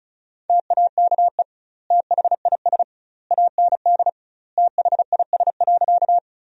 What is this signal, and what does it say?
Morse code, heard as a keyed tone, about 36 wpm, 700 Hz. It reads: TAKE THIS AND THIS.
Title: 2x Word-Spacing - Sentences from Top 200 Words - 35wpm